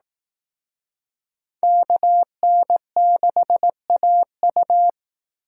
Morse code, heard as a keyed tone, about 18 wpm, 700 Hz. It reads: KN6AU
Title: Speed Racing - USA 2x2 Call Signs 15wpm